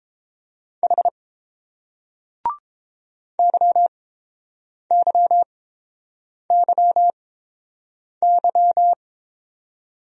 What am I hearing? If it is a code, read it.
Morse code, heard as a keyed tone, about 32 wpm, spaced as for 7 wpm, 700 Hz. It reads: H YYYY